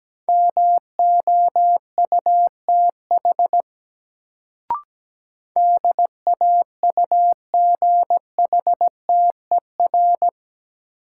Morse code, heard as a keyed tone, about 17 wpm, 700 Hz. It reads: MOUTH DAUGHTER
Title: Sentences from Top 1500 Words - New Word Drill - 17wpm